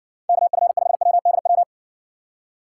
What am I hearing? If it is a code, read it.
Morse code, heard as a keyed tone, about 60 wpm, 700 Hz. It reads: KF5WDW